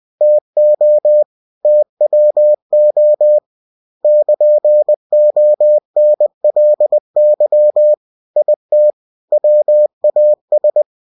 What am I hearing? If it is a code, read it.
Morse code, heard as a keyed tone, about 20 wpm, 600 Hz. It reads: TO TWO <KN>ONLY IT WAS
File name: cw-books / winnie-the-pooh / winnie-20wpm-page-0032